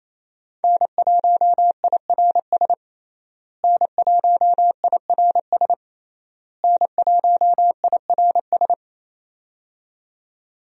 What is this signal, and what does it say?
Morse code, heard as a keyed tone, about 28 wpm, 700 Hz. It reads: N1IRS N1IRS N1IRS
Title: USA Call Signs - 5 Characters - Repeated 3x - 28wpm